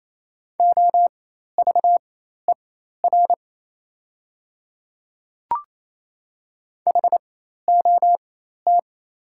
Morse code, heard as a keyed tone, about 28 wpm, spaced as for 13 wpm, 700 Hz. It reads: OVER HOT
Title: 4x Character Spacing - Top 100 Words - 28wpm